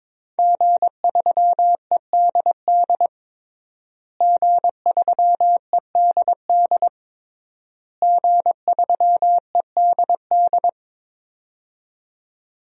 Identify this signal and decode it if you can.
Morse code, heard as a keyed tone, about 21 wpm, 700 Hz. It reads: G3EDD G3EDD G3EDD
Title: DXCC-100 Call Signs - 5 Characters - Repeated 3x - 22wpm